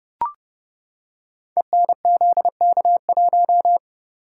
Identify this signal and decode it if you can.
Morse code, heard as a keyed tone, about 30 wpm, 700 Hz. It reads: ENZK1
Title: Code Groups - 5 Letter:Numbers - 30wpm